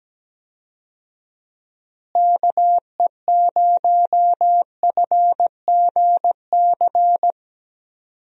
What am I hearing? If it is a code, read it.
Morse code, heard as a keyed tone, about 17 wpm, 700 Hz. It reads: KE0FGC
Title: USA Call Signs - 6 Characters - Repeated 3x - 17wpm